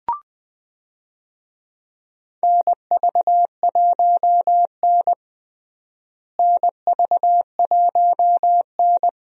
Morse code, heard as a keyed tone, about 20 wpm, 700 Hz. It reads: NV1N NV1N
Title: USA Call Signs - 4 Characters - Repeated 3x - 20wpm